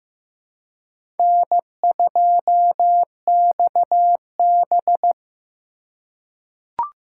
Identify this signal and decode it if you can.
Morse code, heard as a keyed tone, about 15 wpm, 700 Hz. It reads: N2XB
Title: USA Call Signs - 4 Characters - 4x Character Spacing - 15wpm